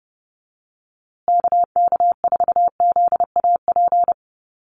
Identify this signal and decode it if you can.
Morse code, heard as a keyed tone, about 30 wpm, 700 Hz. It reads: KK4ZAP
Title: USA Call Signs Repeated 2x 30wpm